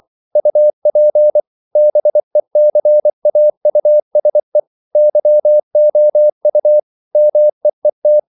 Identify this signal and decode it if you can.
Morse code, heard as a keyed tone, about 24 wpm, 600 Hz. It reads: UP BECAUSE YOU MEET
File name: CW / Practice / EnglishQuotations0224